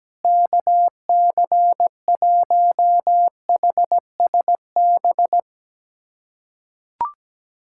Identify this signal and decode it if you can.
Morse code, heard as a keyed tone, about 17 wpm, 700 Hz. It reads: KC1HSB
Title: USA Callsign DE Callsign - 6 Characters - 17wpm